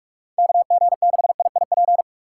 Morse code, heard as a keyed tone, about 47 wpm, 700 Hz. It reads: KG6IIP
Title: K1USN SST QSO Practice - 45wpm